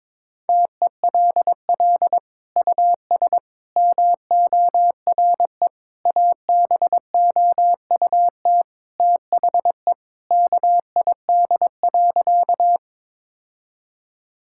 Morse code, heard as a keyed tone, about 21 wpm, 700 Hz. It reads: TELL US MORE ABOUT THE KID.
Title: Sentences from Top 1600 Words 22wpm